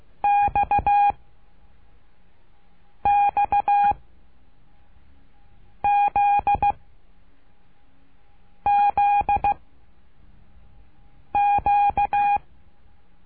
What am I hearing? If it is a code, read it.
Morse code, heard as a keyed tone, about 16 wpm, 810 Hz. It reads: X X Z Z Q
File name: cursos / CURSO DE TELEGRAFIA / Telegrafia - Trilha 12